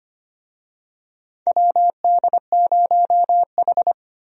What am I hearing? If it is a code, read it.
Morse code, heard as a keyed tone, about 25 wpm, 700 Hz. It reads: WD0H
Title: USA Call Signs - 4 Characters - 25wpm